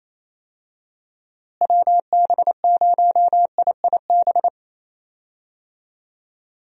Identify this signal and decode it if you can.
Morse code, heard as a keyed tone, about 28 wpm, 700 Hz. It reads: WB0IIB
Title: USA Call Signs - 6 Characters - Repeated 3x - 28wpm